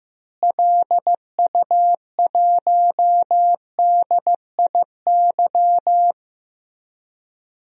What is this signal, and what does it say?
Morse code, heard as a keyed tone, about 15 wpm, 700 Hz. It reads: LU1DIY
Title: DXCC-100 Call Signs - 6 Characters - Repeated 3x - 15wpm